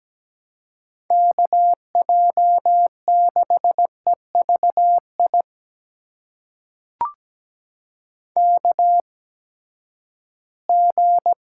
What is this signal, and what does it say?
Morse code, heard as a keyed tone, about 17 wpm, spaced as for 12 wpm, 700 Hz. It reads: KJ6EVI KG
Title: USA Call Signs - 6 Characters - 8x Character Spacing - 17wpm